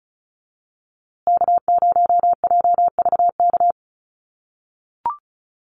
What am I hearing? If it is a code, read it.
Morse code, heard as a keyed tone, about 34 wpm, 700 Hz. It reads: K0JVK
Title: USA Call Signs Repeated 3x 35wpm